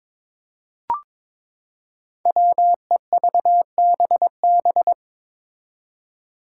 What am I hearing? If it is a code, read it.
Morse code, heard as a keyed tone, about 21 wpm, 700 Hz. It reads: WEVBB
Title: Five Challenging Letter-Number Combinations 22wpm